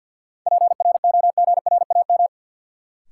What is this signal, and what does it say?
Morse code, heard as a keyed tone, about 46 wpm, 700 Hz. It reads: PROGRAM
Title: Sentences from Top 1500 Words - New Word Drill Alt - 50wpm